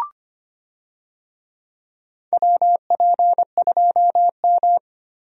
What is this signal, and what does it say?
Morse code, heard as a keyed tone, about 25 wpm, 700 Hz. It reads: WP2M
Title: USA Call Signs - 4 Characters - Repeated 3x - 25wpm